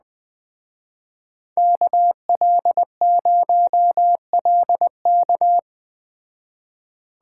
Morse code, heard as a keyed tone, about 21 wpm, 700 Hz. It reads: KL0LK
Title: Speed Racing - USA 2x2 Call Signs 17wpm